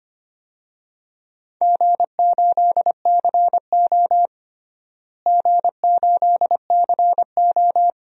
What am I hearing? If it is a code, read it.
Morse code, heard as a keyed tone, about 25 wpm, 700 Hz. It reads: G8CO G8CO
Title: DXCC-100 Call Signs - 4 Characters - Repeated 2x - 25wpm